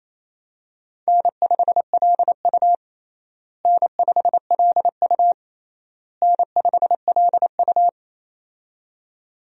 Morse code, heard as a keyed tone, about 27 wpm, 700 Hz. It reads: N5LU N5LU N5LU
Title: USA Call Signs - 4 Characters - Repeated 3x - 28wpm